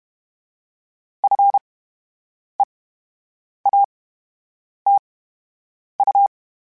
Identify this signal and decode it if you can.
Morse code, heard as a keyed tone, about 32 wpm, spaced as for 8 wpm, 800 Hz. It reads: FEATU
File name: Special INGs / Special INGs-mp3s / VerySlowToFaster Ing @30